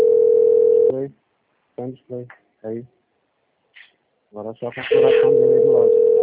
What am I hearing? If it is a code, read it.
Ringback tone (North America), 440 + 480 Hz, with speech.
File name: out- - - - - .68045